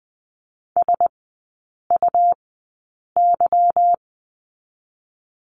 Morse code, heard as a keyed tone, about 20 wpm, spaced as for 8 wpm, 690 Hz. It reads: SUY
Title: 145 Lesson 24e - Letter y - Three Characters 20wpm